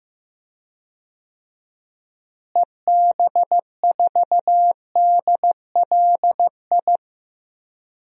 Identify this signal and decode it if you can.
Morse code, heard as a keyed tone, about 15 wpm, 700 Hz. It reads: EB4DLI
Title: DXCC-100 Call Signs - 6 Characters - Repeated 2x - 15wpm